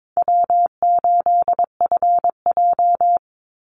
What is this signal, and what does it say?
Morse code, heard as a keyed tone, about 22 wpm, 700 Hz. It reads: W8FJ
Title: Easy Call Signs - 22wpm